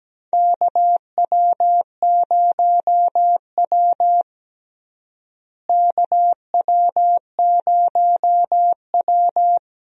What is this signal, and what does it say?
Morse code, heard as a keyed tone, about 17 wpm, 700 Hz. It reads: KW0W KW0W
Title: USA Call Signs - 4 Characters - Repeated 3x - 17wpm